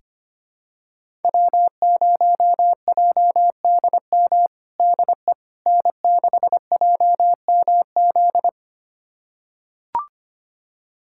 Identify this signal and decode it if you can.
Morse code, heard as a keyed tone, about 25 wpm, 700 Hz. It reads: W0JDM DE N6JMZ
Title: USA Callsign DE Callsign 25wpm